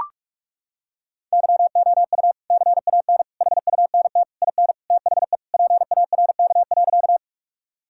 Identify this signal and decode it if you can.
Morse code, heard as a keyed tone, about 41 wpm, 700 Hz. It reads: YOU CAN HUNT IN THE PARK.
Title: Sentences from Top 1600 Words 45wpm